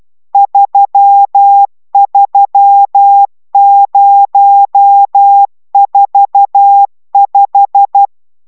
Morse code, heard as a keyed tone, about 12 wpm, 800 Hz. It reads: 33045